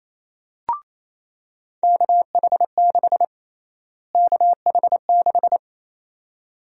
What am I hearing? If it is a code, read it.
Morse code, heard as a keyed tone, about 28 wpm, 700 Hz. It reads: KH6 KH6
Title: USA Call Sign Prefixes - Repeated 2x - 28wpm